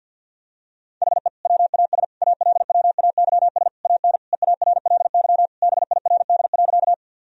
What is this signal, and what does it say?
Morse code, heard as a keyed tone, about 50 wpm, 700 Hz. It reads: HE WAS ALWAYS AN EARLY BIRD.
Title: Sentences from Top 1000 Words - Review Part 2 - 50wpm